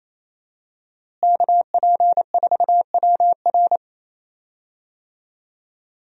Morse code, heard as a keyed tone, about 27 wpm, 700 Hz. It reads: KP4WR
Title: USA Call Signs - 5 Characters - Repeated 2x - 28wpm